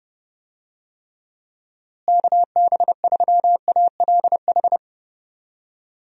Morse code, heard as a keyed tone, about 30 wpm, 700 Hz. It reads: KB3ALH
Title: USA Call Signs - 6 Characters - Repeated 3x - 30wpm